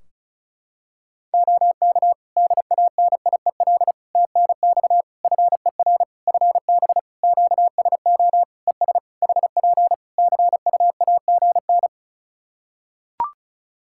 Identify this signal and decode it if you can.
Morse code, heard as a keyed tone, about 36 wpm, 700 Hz. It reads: OK DANIEL TNX FER FB QSO ES HP CUAGN
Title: Practice QSO - Exchange 4 - Ending - 35wpm